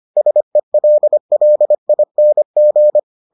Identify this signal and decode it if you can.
Morse code, heard as a keyed tone, about 25 wpm, 600 Hz. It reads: SELLING